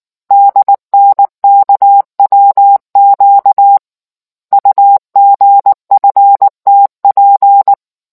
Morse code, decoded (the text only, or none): DNKWQ UGFTP